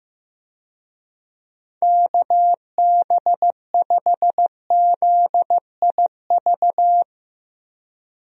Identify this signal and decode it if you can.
Morse code, heard as a keyed tone, about 15 wpm, 700 Hz. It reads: KB5ZIV
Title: USA Call Signs - 6 Characters - Repeated 3x - 15wpm